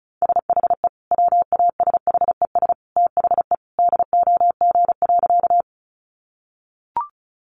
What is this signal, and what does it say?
Morse code, heard as a keyed tone, about 34 wpm, 700 Hz. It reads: SHE WASHES THE DOG.
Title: Easy Phrases 35wpm